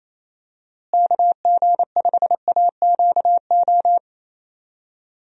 Morse code, heard as a keyed tone, about 28 wpm, 700 Hz. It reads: KG5AQO
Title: USA Call Signs - 6 Characters - Repeated 3x - 28wpm